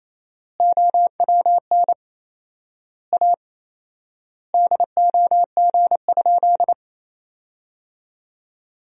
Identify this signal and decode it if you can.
Morse code, heard as a keyed tone, about 28 wpm, 700 Hz. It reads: OWN A DOG?
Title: 4x Word-Spacing - Sentences from Top 400 Words - 28wpm